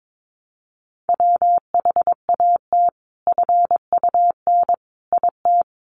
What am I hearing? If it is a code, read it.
Morse code, heard as a keyed tone, about 22 wpm, 700 Hz. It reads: WHAT FUN IT
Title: Sentences from Top 700 Words 22wpm